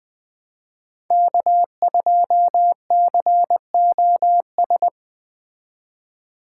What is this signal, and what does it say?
Morse code, heard as a keyed tone, about 20 wpm, 700 Hz. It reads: K2COS